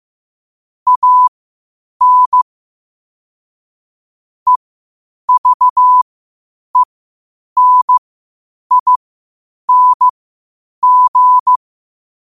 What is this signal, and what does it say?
Morse code, heard as a keyed tone, about 15 wpm, spaced as for 8 wpm, 1000 Hz. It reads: AN EVENING
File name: qotdfiles / 8wpm